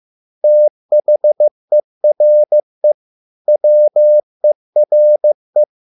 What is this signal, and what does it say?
Morse code, heard as a keyed tone, about 15 wpm, 600 Hz. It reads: THERE WERE